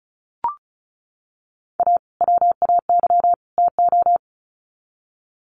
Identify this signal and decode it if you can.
Morse code, heard as a keyed tone, about 34 wpm, 700 Hz. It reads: A WAY TO
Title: Sets of 3 Words 35wpm